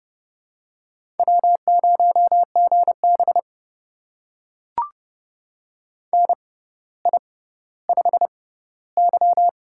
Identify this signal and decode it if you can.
Morse code, heard as a keyed tone, about 32 wpm, spaced as for 16 wpm, 700 Hz. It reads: W0GB NI5Y
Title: USA Call Signs - 4 Characters - 6x Character Spacing - 30wpm